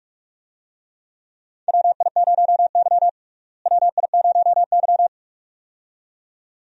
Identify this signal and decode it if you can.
Morse code, heard as a keyed tone, about 42 wpm, 700 Hz. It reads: WI0Y WI0Y
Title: USA Call Signs - 4 Characters - Repeated 2x - 45wpm